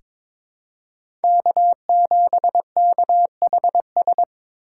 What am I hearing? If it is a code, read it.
Morse code, heard as a keyed tone, about 22 wpm, 700 Hz. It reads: K7KHS
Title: USA Call Signs - 8x Character Spacing - 22wpm